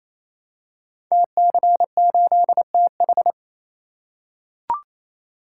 Morse code, heard as a keyed tone, about 28 wpm, 700 Hz. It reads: TC8TH
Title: Code Groups - 5 Letter:Numbers - 28wpm